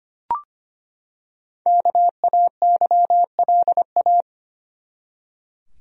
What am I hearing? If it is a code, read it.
Morse code, heard as a keyed tone, about 25 wpm, 700 Hz. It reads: KAYLA
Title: USA Common Names - 25wpm